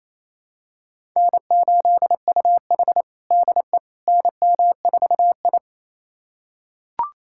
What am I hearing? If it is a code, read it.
Morse code, heard as a keyed tone, about 28 wpm, 700 Hz. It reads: N8UH DE NM4I
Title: USA Callsign DE Callsign - 4 Characters - 28wpm